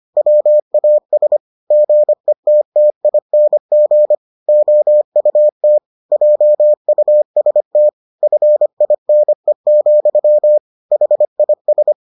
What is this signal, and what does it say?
Morse code, heard as a keyed tone, about 25 wpm, 600 Hz. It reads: WAS GETTING OUT JUST FINE, HIS